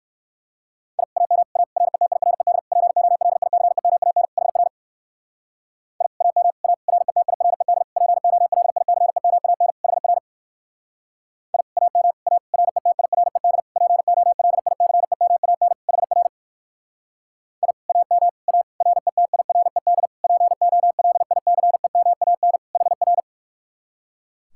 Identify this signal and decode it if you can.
Morse code, heard as a keyed tone, about 59 wpm, 700 Hz. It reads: I AM A RETIRED POLICEMAN HR I AM A RETIRED POLICEMAN HR I AM A RETIRED POLICEMAN HR I AM A RETIRED POLICEMAN HR